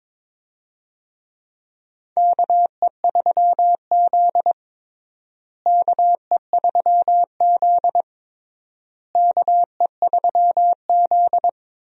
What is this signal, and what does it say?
Morse code, heard as a keyed tone, about 22 wpm, 700 Hz. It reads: KE3Z KE3Z KE3Z
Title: USA Call Signs - 4 Characters - Repeated 3x - 22wpm